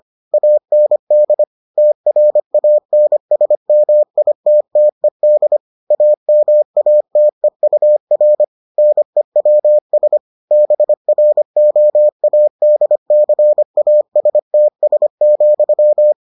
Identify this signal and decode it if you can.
Morse code, heard as a keyed tone, about 25 wpm, 600 Hz. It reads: AND TRANSMITTED AMATEUR NEWS BROADCASTS,